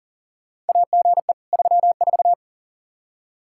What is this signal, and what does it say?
Morse code, heard as a keyed tone, about 40 wpm, 700 Hz. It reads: AGE 34
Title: Practice QSO - Exchange 3 - More Info - 40wpm